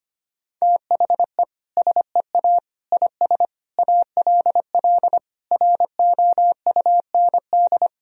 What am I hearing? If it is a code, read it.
Morse code, heard as a keyed tone, about 25 wpm, 700 Hz. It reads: THE SEA IS ALL ROUND
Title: Sentences from Top 1000 Words - Review Part 1 - 25wpm